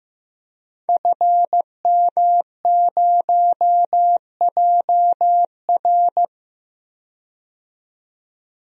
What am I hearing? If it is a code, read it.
Morse code, heard as a keyed tone, about 15 wpm, 700 Hz. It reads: FM0JR
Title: DXCC-100 Call Signs - 5 Characters - 15wpm